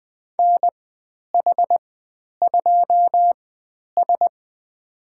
Morse code, heard as a keyed tone, about 20 wpm, spaced as for 10 wpm, 700 Hz. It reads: NH2S
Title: 095 Lesson 16g - Letter c - Call Signs 20wpm